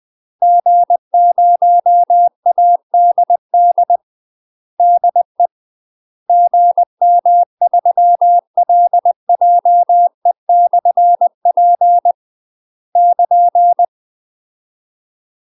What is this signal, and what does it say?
Morse code, heard as a keyed tone, about 20 wpm, 700 Hz. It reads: G0ADD DE GM3LJE/P <KN>
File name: UK MASTER QSOs TRAINING / 20WPM / UK MASTER QSOs 0063